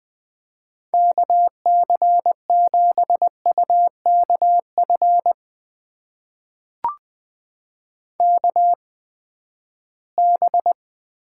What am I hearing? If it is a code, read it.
Morse code, heard as a keyed tone, about 20 wpm, spaced as for 14 wpm, 700 Hz. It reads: KC7UKF KB
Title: USA Call Signs - 6 Characters - 8x Character Spacing - 20wpm